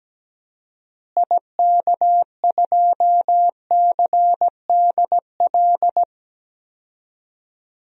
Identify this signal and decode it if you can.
Morse code, heard as a keyed tone, about 17 wpm, 700 Hz. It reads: IK2CDL
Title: DXCC-100 Call Signs - 6 Characters - Repeated 3x - 17wpm